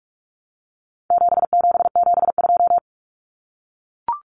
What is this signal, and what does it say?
Morse code, heard as a keyed tone, about 44 wpm, 700 Hz. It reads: 7772
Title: Numbers - 4-Digit 45wpm